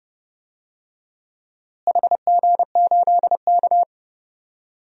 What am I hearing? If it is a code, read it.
Morse code, heard as a keyed tone, about 30 wpm, 700 Hz. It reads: HG8K
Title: DXCC-100 Call Signs - 4 Characters - Repeated 3x - 30wpm